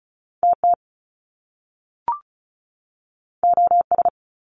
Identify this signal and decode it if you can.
Morse code, heard as a keyed tone, about 36 wpm, 700 Hz. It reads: TT OS
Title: Top Two Letter Combinations in Words 35wpm